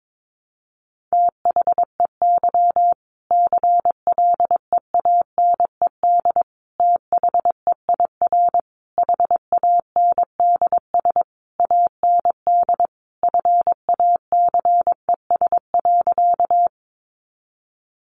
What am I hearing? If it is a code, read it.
Morse code, heard as a keyed tone, about 22 wpm, 700 Hz. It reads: THEY CLEANED THEIR HANDS AND FACES.
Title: A Sound Of Thunder 22wpm